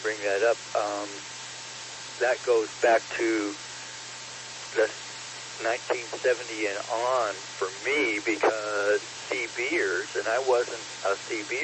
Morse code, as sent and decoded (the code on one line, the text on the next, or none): none